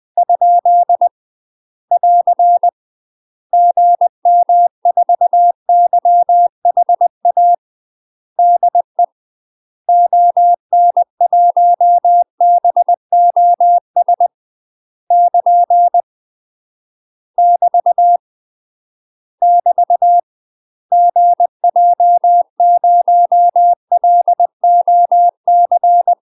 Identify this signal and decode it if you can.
Morse code, heard as a keyed tone, about 20 wpm, 700 Hz. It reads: ? <AR> GM4YHA DE ON1BOS <KN> <BT> <BT> GJ0LOC
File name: UK MASTER QSOs TRAINING/20WPM